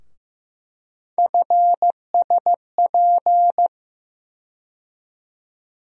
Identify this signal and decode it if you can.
Morse code, heard as a keyed tone, about 15 wpm, 700 Hz. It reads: FSP
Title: USA Call Sign Postfixes - 3 Letters 15wpm